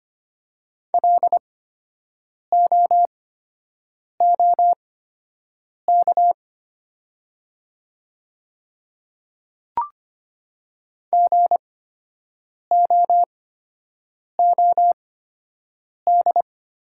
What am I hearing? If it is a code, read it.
Morse code, heard as a keyed tone, about 25 wpm, spaced as for 7 wpm, 700 Hz. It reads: LOOK GOOD
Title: 8x Character Spacing - Top 200 Words - 25wpm